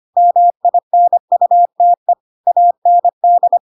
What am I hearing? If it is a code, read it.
Morse code, heard as a keyed tone, about 25 wpm, 700 Hz. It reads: MINUTE AND